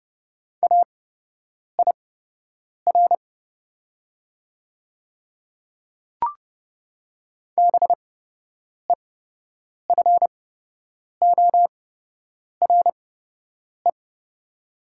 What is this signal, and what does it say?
Morse code, heard as a keyed tone, about 31 wpm, spaced as for 8 wpm, 700 Hz. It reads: AIR BEFORE
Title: 8x Character Spacing - Top 300 Words - 30wpm